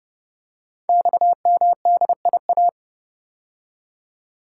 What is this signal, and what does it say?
Morse code, heard as a keyed tone, about 30 wpm, 700 Hz. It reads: XMDIA
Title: Code Groups - 5 Letters - 30wpm